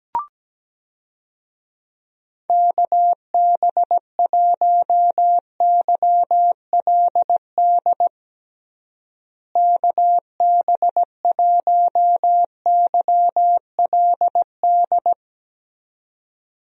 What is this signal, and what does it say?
Morse code, heard as a keyed tone, about 17 wpm, 700 Hz. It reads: KB1YLD KB1YLD
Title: USA Call Signs - 6 Characters - Repeated 2x - 17wpm